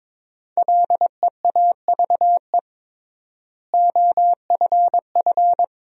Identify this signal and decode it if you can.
Morse code, heard as a keyed tone, about 22 wpm, 700 Hz. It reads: LEAVE OFF
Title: 3x Word-Spacing - Sentences from Top 400 Words - 22wpm